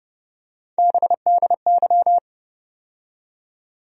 Morse code, heard as a keyed tone, about 30 wpm, 700 Hz. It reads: BDY
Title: USA Call Sign Postfixes - 3 Letters 30wpm